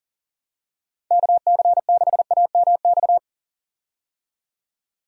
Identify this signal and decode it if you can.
Morse code, heard as a keyed tone, about 39 wpm, 700 Hz. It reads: KC6AMX
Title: USA Call Signs - 6 Characters - 40wpm